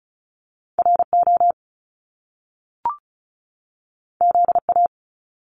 Morse code, heard as a keyed tone, about 34 wpm, 700 Hz. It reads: RO ZA
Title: Top Two Letter Combinations in Words 35wpm